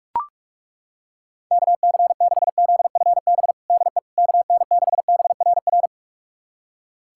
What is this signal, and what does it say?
Morse code, heard as a keyed tone, about 41 wpm, 700 Hz. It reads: KC6ZFB DE KN6BRR
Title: USA Callsign DE Callsign 45wpm